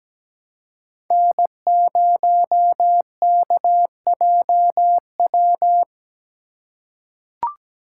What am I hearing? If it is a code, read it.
Morse code, heard as a keyed tone, about 17 wpm, 700 Hz. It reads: N0KJW
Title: USA Call Signs - 5 Characters - 4x Character Spacing - 17wpm